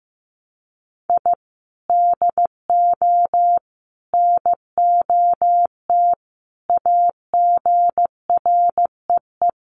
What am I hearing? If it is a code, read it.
Morse code, heard as a keyed tone, about 15 wpm, 700 Hz. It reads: I DO NOT AGREE
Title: Sets of 5 Words Encore 15wpm